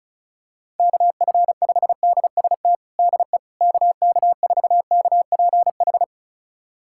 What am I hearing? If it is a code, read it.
Morse code, heard as a keyed tone, about 34 wpm, 700 Hz. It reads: KF5DST DE KK4KPH
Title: USA Callsign DE Callsign 35wpm